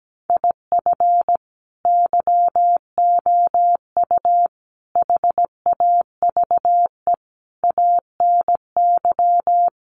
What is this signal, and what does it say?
Morse code, heard as a keyed tone, about 17 wpm, 700 Hz. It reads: IF YOU HAVE ANY
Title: Sentences from Top 1100 Words 17wpm